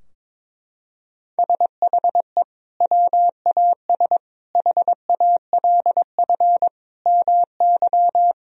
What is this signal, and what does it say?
Morse code, heard as a keyed tone, about 22 wpm, 700 Hz. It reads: SHE WAS HALF MY
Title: Sentences from Top 1000 Words - Review Part 2 - 22wpm